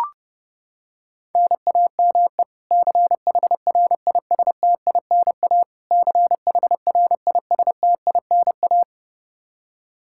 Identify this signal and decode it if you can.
Morse code, heard as a keyed tone, about 30 wpm, 700 Hz. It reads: NAME CHRISTINA CHRISTINA
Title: K1USN SST QSO Practice 30wpm